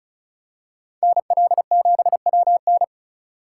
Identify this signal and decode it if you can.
Morse code, heard as a keyed tone, about 36 wpm, 700 Hz. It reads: NL7WN
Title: USA Call Signs - 5 Characters - 2x Character Spacing - 35wpm